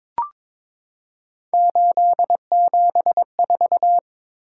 Morse code, heard as a keyed tone, about 22 wpm, 700 Hz. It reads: 874
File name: practice / 22wpm / Three Letter-Number Combinations - 22wpm